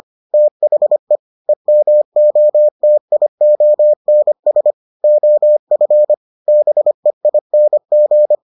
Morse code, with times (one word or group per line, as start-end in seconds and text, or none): 0.3-1.2 THE
1.5-4.7 EMOTIONS
5.0-6.1 OF
6.5-8.4 BEING